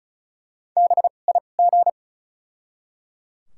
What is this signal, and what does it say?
Morse code, heard as a keyed tone, about 34 wpm, spaced as for 25 wpm, 700 Hz. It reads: BIG
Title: 2x Character Spacing - Top 300 Words - 35wpm